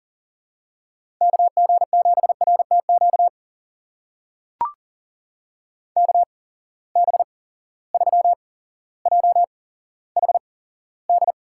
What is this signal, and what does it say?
Morse code, heard as a keyed tone, about 40 wpm, spaced as for 17 wpm, 700 Hz. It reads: KG7RTQ KB3JHD
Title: USA Call Signs - 6 Characters - 8x Character Spacing - 40wpm